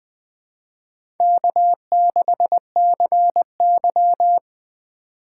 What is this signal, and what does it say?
Morse code, heard as a keyed tone, about 20 wpm, 700 Hz. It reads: K6CY
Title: USA Call Signs - 4 Characters - 3x Character Spacing - 20wpm